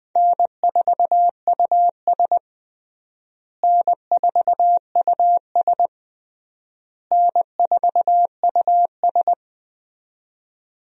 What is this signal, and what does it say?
Morse code, heard as a keyed tone, about 20 wpm, 700 Hz. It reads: N4US N4US N4US
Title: USA Call Signs - 4 Characters - Repeated 3x - 20wpm